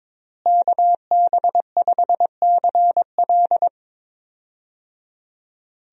Morse code, heard as a keyed tone, about 22 wpm, 700 Hz. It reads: KB5CL
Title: USA Call Signs - 5 Characters - Repeated 3x - 22wpm